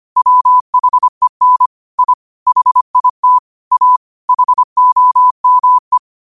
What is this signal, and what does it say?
Morse code, heard as a keyed tone, about 25 wpm, 1000 Hz. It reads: WHEN I HIT A HOME
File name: qotdfiles / 25wpm